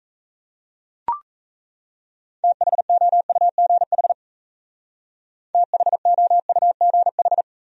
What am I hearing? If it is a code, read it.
Morse code, heard as a keyed tone, about 38 wpm, 700 Hz. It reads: THOUGH THOUGH